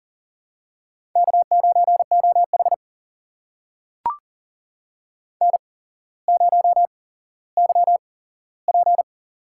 Morse code, heard as a keyed tone, about 39 wpm, spaced as for 17 wpm, 700 Hz. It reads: K9OH N0YP
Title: USA Call Signs - 4 Characters - 8x Character Spacing - 40wpm